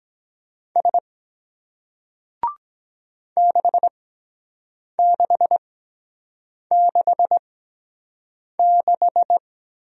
Morse code, heard as a keyed tone, about 23 wpm, spaced as for 7 wpm, 700 Hz. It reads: S 6666